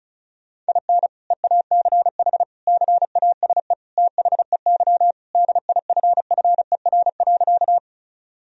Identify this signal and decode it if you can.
Morse code, heard as a keyed tone, about 34 wpm, 700 Hz. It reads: IN EACH CASE THEY DIFFER.
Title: Sentences from Top 1000 Words - Review Part 1 - 35wpm